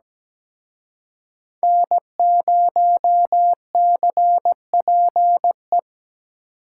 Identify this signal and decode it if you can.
Morse code, heard as a keyed tone, about 20 wpm, 700 Hz. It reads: N0CPE
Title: Speed Racing - USA 1x3 Call Signs 17wpm